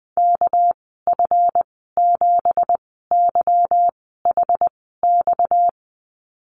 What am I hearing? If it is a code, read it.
Morse code, heard as a keyed tone, about 20 wpm, spaced as for 15 wpm, 700 Hz. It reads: KF7YHX
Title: 245 Lesson 39g - Letter z - Call Signs 20wpm